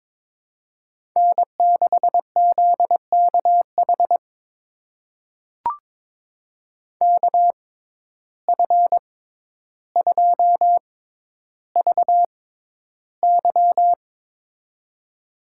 Morse code, heard as a keyed tone, about 22 wpm, spaced as for 11 wpm, 700 Hz. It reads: N6ZKH KF2VY
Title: USA Call Signs - 5 Characters - 6x Character Spacing - 22wpm